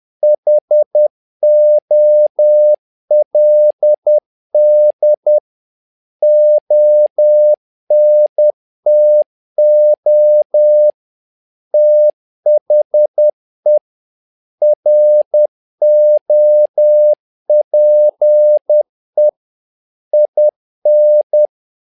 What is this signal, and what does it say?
Morse code, heard as a keyed tone, about 10 wpm, 600 Hz. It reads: HOLD ONTO THE ROPE IN
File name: learning center / tower trouble / tower trouble 10wpm